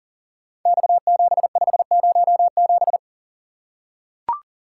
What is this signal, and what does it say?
Morse code, heard as a keyed tone, about 38 wpm, 700 Hz. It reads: X7507
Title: Code Groups - 5 Letter:Numbers - 40wpm